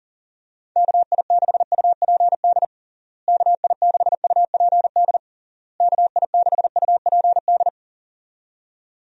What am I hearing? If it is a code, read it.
Morse code, heard as a keyed tone, about 40 wpm, 700 Hz. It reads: KI6UPD KI6UPD KI6UPD
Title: USA Call Signs - 6 Characters - Repeated 3x - 40wpm